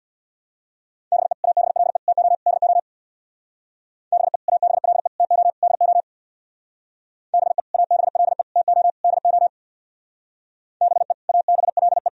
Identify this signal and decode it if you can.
Morse code, heard as a keyed tone, about 64 wpm, 700 Hz. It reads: BE ABLE TO DO BE ABLE TO DO BE ABLE TO DO BE ABLE